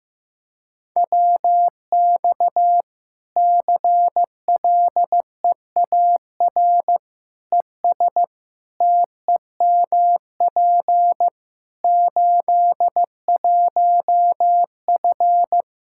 Morse code, heard as a keyed tone, about 16 wpm, 700 Hz. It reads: WX CLEAR ES TEMP 81F
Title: Practice QSO - Exchange 2 - Info - 15wpm